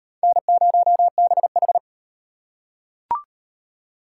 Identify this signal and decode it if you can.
Morse code, heard as a keyed tone, about 35 wpm, 700 Hz. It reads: N0BH